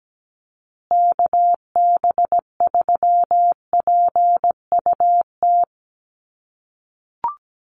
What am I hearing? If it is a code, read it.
Morse code, heard as a keyed tone, about 17 wpm, 700 Hz. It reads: KB3PUT
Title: USA Call Signs Repeated 3x 17wpm